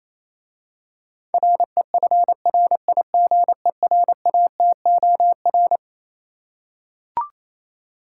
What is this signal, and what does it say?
Morse code, heard as a keyed tone, about 27 wpm, 700 Hz. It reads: REFRIGERATOR
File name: Easy Words - Speed-Racing with Hints - 22wpm